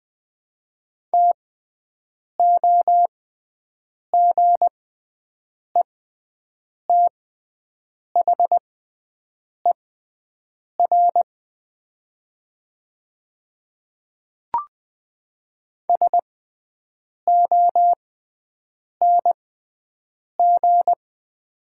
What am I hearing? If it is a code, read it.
Morse code, heard as a keyed tone, about 20 wpm, spaced as for 7 wpm, 700 Hz. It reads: TOGETHER SONG
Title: 6x Character Spacing - Top 500 Words - 20wpm